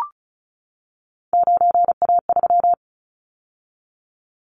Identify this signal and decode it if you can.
Morse code, heard as a keyed tone, about 36 wpm, 700 Hz. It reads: 9A3
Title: Callsign Prefixes for Top 25 DXCC - 35wpm